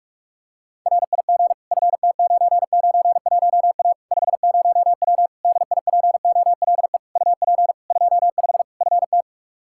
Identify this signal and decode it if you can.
Morse code, heard as a keyed tone, about 41 wpm, 700 Hz. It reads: RIG FT991A 50W DIPOLE UP 25 FT
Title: Practice QSO - Exchange 2 - Info - 45wpm